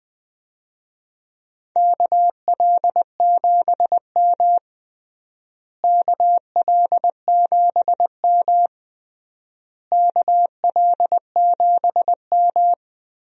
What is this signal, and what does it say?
Morse code, heard as a keyed tone, about 20 wpm, 700 Hz. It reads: KL7M KL7M KL7M
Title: DXCC-100 Call Signs - 4 Characters - Repeated 3x - 20wpm